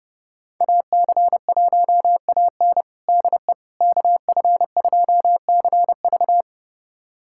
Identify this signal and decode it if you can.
Morse code, heard as a keyed tone, about 30 wpm, 700 Hz. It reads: AC1AN DE KF2CV
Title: USA Callsign DE Callsign - 5 Characters - 30wpm